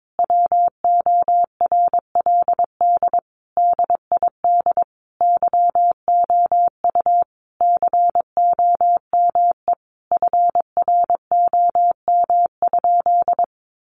Morse code, heard as a keyed tone, about 22 wpm, 700 Hz. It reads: WORLD DID YOU COME FROM?
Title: Sentences from Top 200 Words 22wpm